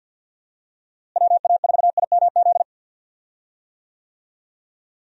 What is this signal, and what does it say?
Morse code, heard as a keyed tone, about 51 wpm, 700 Hz. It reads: WA4IMZ
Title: USA Call Signs - 6 Characters - Repeated 3x - 50wpm